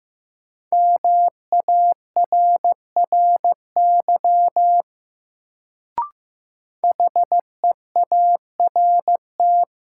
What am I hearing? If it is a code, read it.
Morse code, heard as a keyed tone, about 15 wpm, 700 Hz. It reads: MARRY HEART
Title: Sentences from Top 1200 Words - New Word Drill - 15wpm